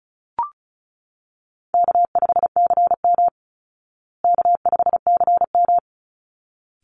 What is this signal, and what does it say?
Morse code, heard as a keyed tone, about 34 wpm, 700 Hz. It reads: K5CM K5CM
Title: Easy Call Signs Repeated 2x - 35wpm